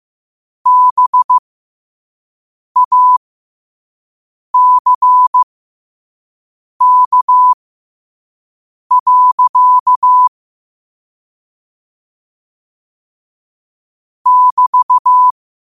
Morse code, heard as a keyed tone, about 15 wpm, spaced as for 5 wpm, 1000 Hz. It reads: BACK. <BT>
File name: qotdfiles / 5wpm